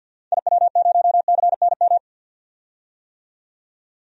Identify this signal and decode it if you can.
Morse code, heard as a keyed tone, about 45 wpm, 700 Hz. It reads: IW0CNM
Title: DXCC-100 Call Signs - 6 Characters - 50wpm